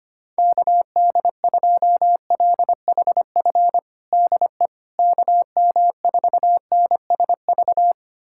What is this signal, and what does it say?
Morse code, heard as a keyed tone, about 25 wpm, 700 Hz. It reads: KD2LHF DE KM4NSV
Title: USA Callsign DE Callsign - 6 Characters - 25wpm